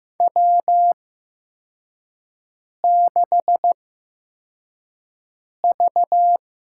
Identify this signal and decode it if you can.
Morse code, heard as a keyed tone, about 15 wpm, spaced as for 4 wpm, 700 Hz. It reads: W6V
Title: USA Call Signs - 8x Character Spacing - 15wpm